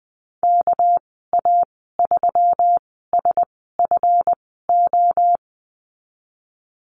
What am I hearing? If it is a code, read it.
Morse code, heard as a keyed tone, about 20 wpm, spaced as for 15 wpm, 700 Hz. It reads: KA3SFO
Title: 207 Lesson 33g - Letter j - Call Signs 20wpm